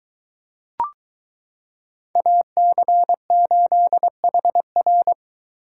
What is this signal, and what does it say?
Morse code, heard as a keyed tone, about 22 wpm, 700 Hz. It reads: AC8HR